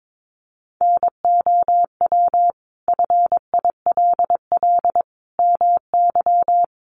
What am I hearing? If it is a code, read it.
Morse code, heard as a keyed tone, about 22 wpm, 700 Hz. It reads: NOW FILL MY
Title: Sentences from Top 1500 Words 22wpm